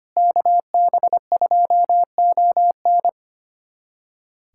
Morse code, heard as a keyed tone, about 25 wpm, 700 Hz. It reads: KB2ON
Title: USA Callsign DE Callsign - 5 Characters - 25wpm